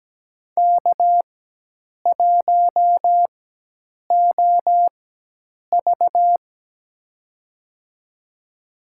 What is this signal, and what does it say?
Morse code, heard as a keyed tone, about 17 wpm, spaced as for 8 wpm, 700 Hz. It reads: K1OV
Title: USA Call Signs - 4 Characters - 4x Character Spacing - 17wpm